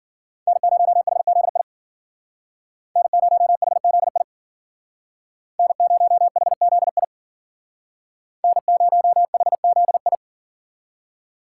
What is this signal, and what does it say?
Morse code, heard as a keyed tone, about 51 wpm, 700 Hz. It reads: N0HZI N0HZI N0HZI N0HZI